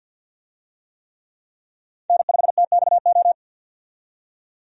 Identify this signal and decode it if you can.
Morse code, heard as a keyed tone, about 52 wpm, 700 Hz. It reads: N5TXO
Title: USA Call Signs - 5 Characters - 50wpm